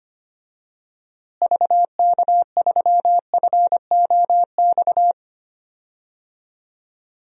Morse code, heard as a keyed tone, about 25 wpm, 700 Hz. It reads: VK3FOX
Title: DXCC-100 Call Signs - 6 Characters - Repeated 3x - 25wpm